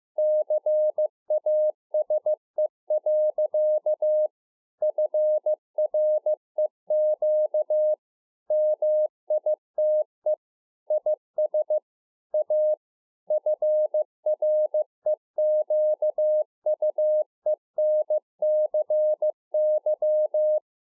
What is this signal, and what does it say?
Morse code, heard as a keyed tone, about 15 wpm, 610 Hz. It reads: CASE. FREQ MITE IS A FREQUENCY
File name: 2011-04-mp3 / A SMALL TRANCEIVER 2 15WPM